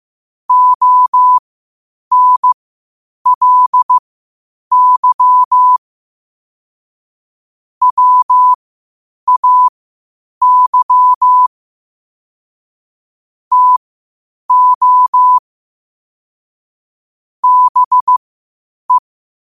Morse code, heard as a keyed tone, about 15 wpm, spaced as for 8 wpm, 1000 Hz. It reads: ONLY WAY TO BE